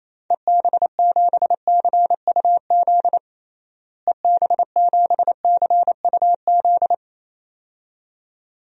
Morse code, heard as a keyed tone, about 28 wpm, 700 Hz. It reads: EB7CUZ EB7CUZ
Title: DXCC-100 Call Signs - 6 Characters - Repeated 2x - 28wpm